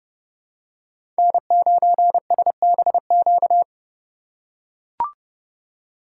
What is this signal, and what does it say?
Morse code, heard as a keyed tone, about 30 wpm, 700 Hz. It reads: N9SBQ
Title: USA Call Signs - 5 Characters - 3x Character Spacing - 30wpm